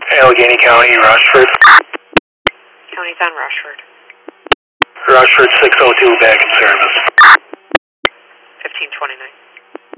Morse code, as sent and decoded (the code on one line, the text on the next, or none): none